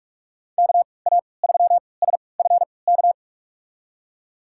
Morse code, heard as a keyed tone, about 41 wpm, spaced as for 25 wpm, 700 Hz. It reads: KA3SFK
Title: USA Call Signs - 3x Character Spacing - 45wpm